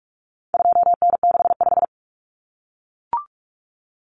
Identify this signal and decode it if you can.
Morse code, heard as a keyed tone, about 44 wpm, 700 Hz. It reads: 2N65
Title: Common Cut Numbers - 4-Digit 45wpm